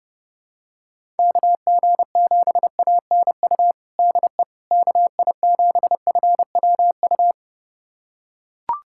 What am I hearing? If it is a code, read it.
Morse code, heard as a keyed tone, about 32 wpm, 700 Hz. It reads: KG7ANU DE KI7FWU
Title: USA Callsign DE Callsign - 6 Characters - 30wpm